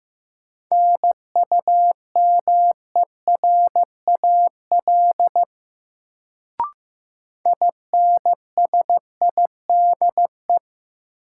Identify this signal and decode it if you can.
Morse code, heard as a keyed tone, about 15 wpm, 700 Hz. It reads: NUMERAL INSIDE
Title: Sentences from Top 1000 Words - Review Part 2 - New Word Drill - 15wpm